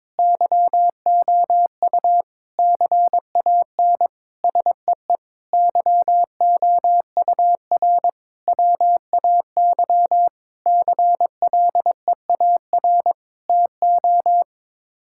Morse code, heard as a keyed tone, about 22 wpm, 700 Hz. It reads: YOU CAN SEE YOUR WAY CLEAR TO
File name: Sets of 9 Words - 22wpm